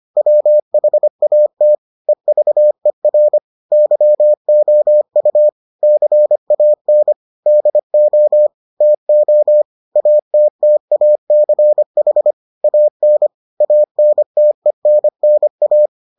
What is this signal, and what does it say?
Morse code, heard as a keyed tone, about 25 wpm, 600 Hz. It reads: WHAT EVER YOU CAN DO TO ATTACH AN ANTENNA